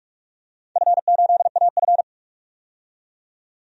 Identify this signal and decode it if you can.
Morse code, heard as a keyed tone, about 44 wpm, 700 Hz. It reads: F8AF
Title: DXCC-100 Call Signs - 4 Characters - 45wpm